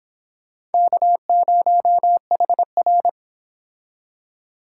Morse code, heard as a keyed tone, about 30 wpm, 700 Hz. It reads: K0HR